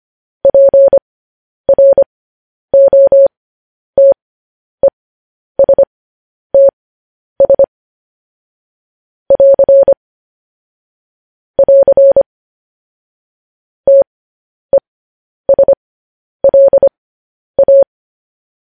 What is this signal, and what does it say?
Morse code, heard as a keyed tone, about 25 wpm, spaced as for 10 wpm, 550 Hz. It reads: PROTESTS <AR> <AR> TESLA
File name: podcast / News headlines-1x-25x10